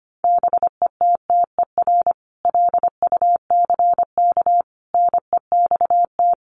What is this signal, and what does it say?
Morse code, heard as a keyed tone, about 25 wpm, 700 Hz. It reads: BETTER LUCK NEXT
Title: Sentences from Top 1500 Words 25wpm